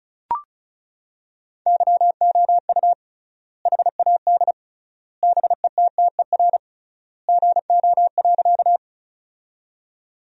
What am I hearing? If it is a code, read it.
Morse code, heard as a keyed tone, about 34 wpm, 700 Hz. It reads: YOU HAD BETTER GO.
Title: 3x Word-Spacing - Sentences from Top 400 Words - 35wpm